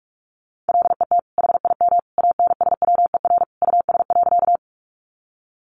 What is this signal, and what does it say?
Morse code, heard as a keyed tone, about 45 wpm, 700 Hz. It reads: LET HIM ANSWER US.
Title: Sentences from Top 300 Words 45wpm